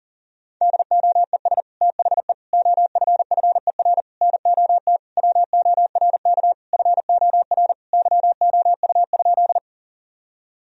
Morse code, heard as a keyed tone, about 40 wpm, 700 Hz. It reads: DOES THE OFFER NOT WORK FOR YOU?
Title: Sentences from Top 1000 Words - Review Part 4 - 40wpm